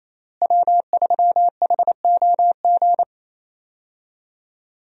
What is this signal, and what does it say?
Morse code, heard as a keyed tone, about 27 wpm, 700 Hz. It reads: W3HOG
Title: USA Call Signs - 5 Characters - Repeated 3x - 28wpm